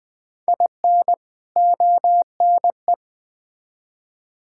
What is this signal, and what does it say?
Morse code, heard as a keyed tone, about 20 wpm, 700 Hz. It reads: IN ONE
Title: Build A Phrase - 3-Word Phrases - 20wpm